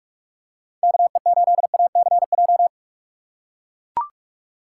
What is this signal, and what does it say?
Morse code, heard as a keyed tone, about 42 wpm, 700 Hz. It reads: KE8ACJ
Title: USA Call Signs - 6 Characters - 3x Character Spacing - 45wpm